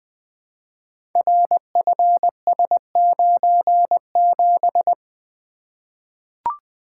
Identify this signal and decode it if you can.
Morse code, heard as a keyed tone, about 20 wpm, 700 Hz. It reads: RFS97
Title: Code Groups - 5 Letter:Numbers - 20wpm